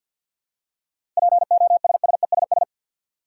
Morse code, heard as a keyed tone, about 50 wpm, 700 Hz. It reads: POSSESS
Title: Seven Letter Words - 50wpm